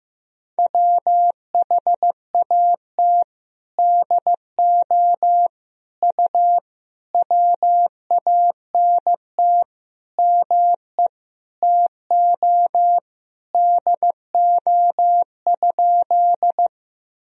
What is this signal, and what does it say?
Morse code, heard as a keyed tone, about 16 wpm, 700 Hz. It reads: WHAT DO U WANT ME TO DO?
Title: Sentences from Top 200 Words and Abbreviations 15wpm